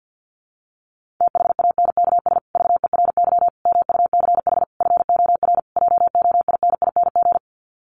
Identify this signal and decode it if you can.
Morse code, heard as a keyed tone, about 50 wpm, 700 Hz. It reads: THANKS VERY MUCH FOR JOINING
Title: Sets of 5 Words Encore2 50wpm